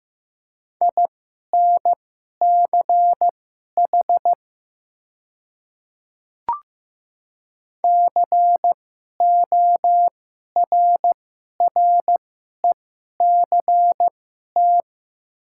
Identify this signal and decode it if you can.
Morse code, heard as a keyed tone, about 15 wpm, spaced as for 11 wpm, 700 Hz. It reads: INCH CORRECT
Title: 2x Character Spacing - Top 500 Words - 15wpm